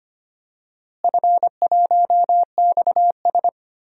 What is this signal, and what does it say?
Morse code, heard as a keyed tone, about 25 wpm, 700 Hz. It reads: F1XS
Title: DXCC-100 Call Signs - 4 Characters - Repeated 2x - 25wpm